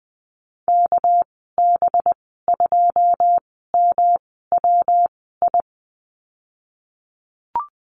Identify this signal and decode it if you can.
Morse code, heard as a keyed tone, about 20 wpm, spaced as for 15 wpm, 700 Hz. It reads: KB2MWI
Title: 219 Lesson 35g - Number 0 - Call Signs 20wpm